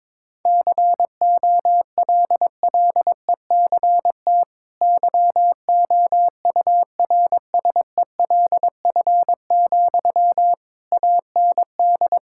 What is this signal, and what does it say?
Morse code, heard as a keyed tone, about 22 wpm, 700 Hz. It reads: COLLECT YOURSELF, AND
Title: Sentences from Top 1600 Words 22wpm